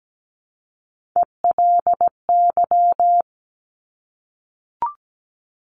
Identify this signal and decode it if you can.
Morse code, heard as a keyed tone, about 17 wpm, 700 Hz. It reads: ELY
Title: Top Three Letter Combinations in Words 17wpm